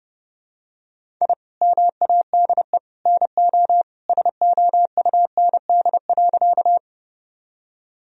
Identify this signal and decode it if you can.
Morse code, heard as a keyed tone, about 30 wpm, 700 Hz. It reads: I MADE NO SOUND.
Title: Sentences from Top 1000 Words - Review Part 1 - 30wpm